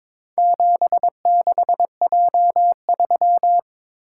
Morse code, heard as a keyed tone, about 22 wpm, 700 Hz. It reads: 76J3
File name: practice / 22wpm / Four Letter-Number Combinations - 22wpm